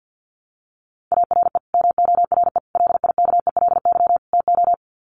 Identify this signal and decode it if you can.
Morse code, heard as a keyed tone, about 50 wpm, 700 Hz. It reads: ARE MORE LIKELY TO
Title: Sets of 4 Words 50wpm